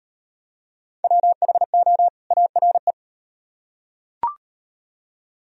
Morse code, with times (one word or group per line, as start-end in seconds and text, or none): 1.0-2.1 WHO
2.3-2.9 ARE